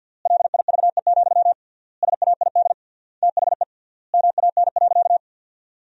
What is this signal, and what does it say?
Morse code, heard as a keyed tone, about 48 wpm, 700 Hz. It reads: LIVE, SAID THE MAN.
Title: 3x Word-Spacing - Sentences from Top 200 Words - 50wpm